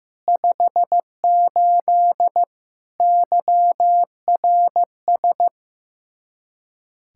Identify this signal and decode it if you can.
Morse code, heard as a keyed tone, about 15 wpm, 700 Hz. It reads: 58 YRS
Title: Practice QSO - Exchange 3 - More Info - 15wpm